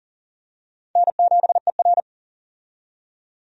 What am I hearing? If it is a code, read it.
Morse code, heard as a keyed tone, about 40 wpm, 700 Hz. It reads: N7ER
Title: USA Call Signs - 4 Characters - 40wpm